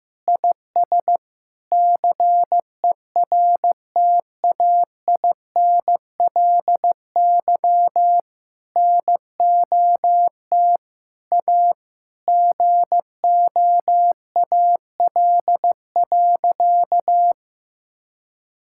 Morse code, heard as a keyed tone, about 15 wpm, 700 Hz. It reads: IS CERTAINLY NOT A GOAL.
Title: Sentences from Top 1000 Words - Review Part 5 - 15wpm